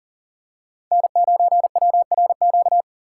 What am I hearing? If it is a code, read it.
Morse code, heard as a keyed tone, about 40 wpm, 700 Hz. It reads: N9WRQ
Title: USA Call Signs - 5 Characters - 4x Character Spacing - 40wpm